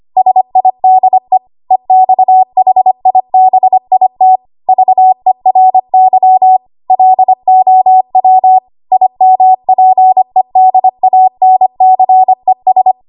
Morse code, heard as a keyed tone, about 25 wpm, 750 Hz. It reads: SIDE EXHIBIT VERY LOW IMPEDANCES
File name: Archive / 25 WPM / 25WPM